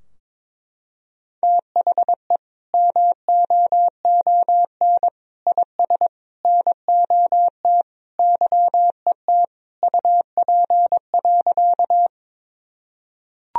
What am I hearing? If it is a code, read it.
Morse code, heard as a keyed tone, about 22 wpm, 700 Hz. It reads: THE MOON IS NOT YET UP.
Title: Sentences from Top 1000 Words - Review Part 3 - 22wpm